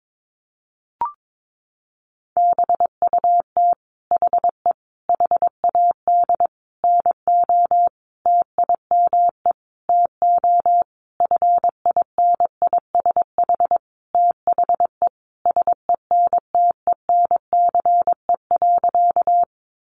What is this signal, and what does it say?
Morse code, heard as a keyed tone, about 22 wpm, 700 Hz. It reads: BUT HE HAD NO TIME TO FINISH THE SENTENCE.
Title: Sentences from Top 1200 Words 22wpm